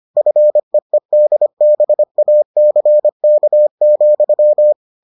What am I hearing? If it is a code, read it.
Morse code, heard as a keyed tone, about 25 wpm, 600 Hz. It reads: FEEDBACK,